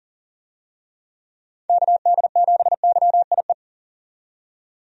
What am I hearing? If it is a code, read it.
Morse code, heard as a keyed tone, about 40 wpm, 700 Hz. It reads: KD7YIE
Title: USA Call Signs - 6 Characters - 40wpm